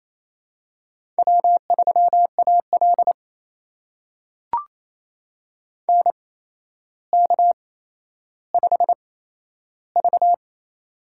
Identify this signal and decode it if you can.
Morse code, heard as a keyed tone, about 28 wpm, spaced as for 12 wpm, 700 Hz. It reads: W3AL NK5V
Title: USA Call Signs - 4 Characters - 8x Character Spacing - 28wpm